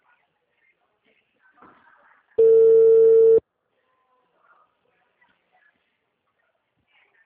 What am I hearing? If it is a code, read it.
Ringback tone (China, 450 Hz).